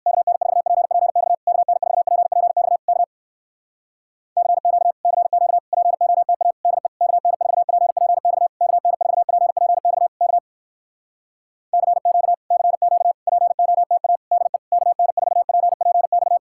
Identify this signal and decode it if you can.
Morse code, heard as a keyed tone, about 55 wpm, 700 Hz. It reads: KN4PPX KN4PPX K CQ CQ POTA DE KN4PPX KN4PPX K CQ CQ POTA DE KN4PPX
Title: Speed-Racing Common QSO Phrases - 45wpm